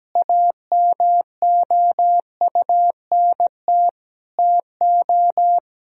Morse code, heard as a keyed tone, about 17 wpm, 700 Hz. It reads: AMOUNT TO
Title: Sentences from Top 1000 Words - Review Part 4 - 17wpm